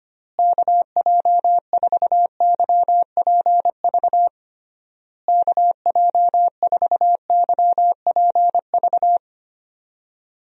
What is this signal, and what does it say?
Morse code, heard as a keyed tone, about 25 wpm, 700 Hz. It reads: KJ4YPV KJ4YPV
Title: USA Call Signs - 6 Characters - Repeated 3x - 25wpm